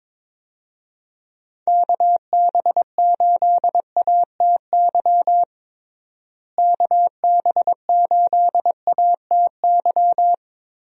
Morse code, heard as a keyed tone, about 22 wpm, 700 Hz. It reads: KB8ATY KB8ATY
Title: USA Call Signs - 6 Characters - Repeated 2x - 22wpm